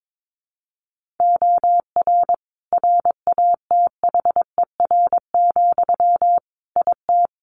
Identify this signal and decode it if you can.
Morse code, heard as a keyed tone, about 22 wpm, 700 Hz. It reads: OR RATHER, IT
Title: Sentences from Top 1400 Words 22wpm